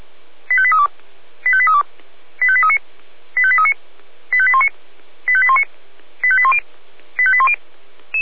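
Five-tone selective call (ZVEI), heard as quick runs of five tones, tones of about 70 ms. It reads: 85832 85832 85839 85839 85819 85819 85810 85810
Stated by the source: ZVEI five-tone call received by SDR, from feuerwehr-orlamuende.org fire brigade audio server.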